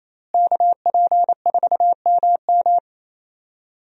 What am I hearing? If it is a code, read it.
Morse code, heard as a keyed tone, about 28 wpm, 700 Hz. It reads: KP4MM
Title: USA Call Signs - 5 Characters - Repeated 2x - 28wpm